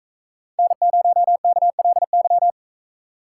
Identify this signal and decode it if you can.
Morse code, heard as a keyed tone, about 42 wpm, 700 Hz. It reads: N0KLY